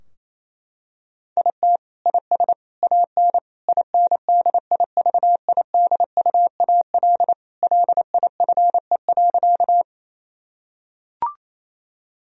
Morse code, heard as a keyed tone, about 28 wpm, 700 Hz. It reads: IT IS AN INDIVIDUAL LIFE.
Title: Sentences from Top 1000 Words - Review Part 4 - 28wpm